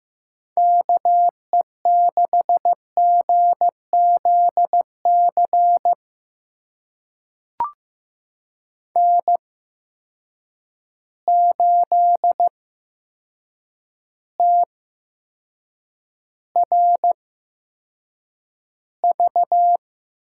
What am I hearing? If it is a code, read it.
Morse code, heard as a keyed tone, about 15 wpm, spaced as for 7 wpm, 700 Hz. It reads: KE6GZC N8TRV
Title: USA Call Signs - 8x Character Spacing - 15wpm